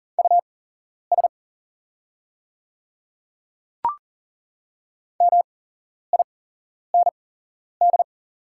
Morse code, heard as a keyed tone, about 40 wpm, spaced as for 11 wpm, 700 Hz. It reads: US MIND
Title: 8x Character Spacing - Top 500 Words - 40wpm